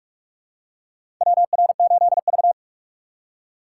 Morse code, heard as a keyed tone, about 44 wpm, 700 Hz. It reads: WR8V
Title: USA Call Signs - 4 Characters - 3x Character Spacing - 45wpm